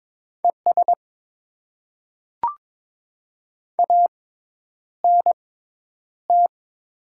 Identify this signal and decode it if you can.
Morse code, heard as a keyed tone, about 22 wpm, spaced as for 10 wpm, 700 Hz. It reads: ES ANT
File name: Top 20 QSO Elements - 6x Character Spacing - 22wpm